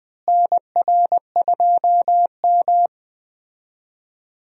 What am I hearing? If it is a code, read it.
Morse code, heard as a keyed tone, about 21 wpm, 700 Hz. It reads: NR2M
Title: USA Call Signs - 4 Characters - Repeated 3x - 20wpm